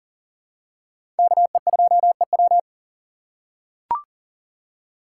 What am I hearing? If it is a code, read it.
Morse code, heard as a keyed tone, about 39 wpm, 700 Hz. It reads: KE2EW
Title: USA Call Signs - 5 Characters - 3x Character Spacing - 40wpm